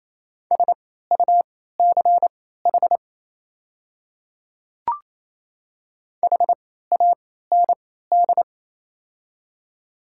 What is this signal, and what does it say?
Morse code, heard as a keyed tone, about 28 wpm, spaced as for 16 wpm, 700 Hz. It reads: SUCH HAND
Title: 3x Character Spacing - Top 200 Words - 28wpm